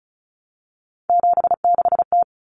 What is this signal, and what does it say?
Morse code, heard as a keyed tone, about 34 wpm, 700 Hz. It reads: 76T
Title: Common Cut Numbers - 3-Digit 35wpm